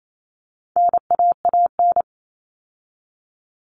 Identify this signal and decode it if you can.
Morse code, heard as a keyed tone, about 28 wpm, 700 Hz. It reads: NAAN
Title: All Cut Numbers - 4-Digit 28wpm